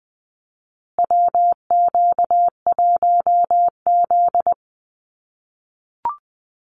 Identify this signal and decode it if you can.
Morse code, heard as a keyed tone, about 20 wpm, 700 Hz. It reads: WQ1Z
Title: USA Call Signs Repeated 2x 20wpm